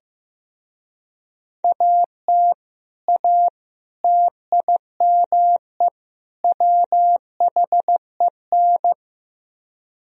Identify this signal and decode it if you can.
Morse code, heard as a keyed tone, about 16 wpm, 700 Hz. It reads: AT A TIME WHEN
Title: Speed-Racing - Sets of 4 Words 15wpm